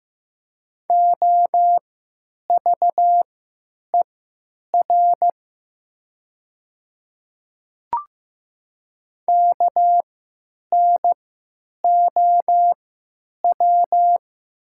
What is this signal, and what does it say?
Morse code, heard as a keyed tone, about 15 wpm, spaced as for 9 wpm, 700 Hz. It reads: OVER KNOW
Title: 3x Character Spacing - Top 100 Words - 15wpm